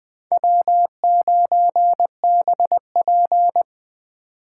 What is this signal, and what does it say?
Morse code, heard as a keyed tone, about 20 wpm, 700 Hz. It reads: W9BP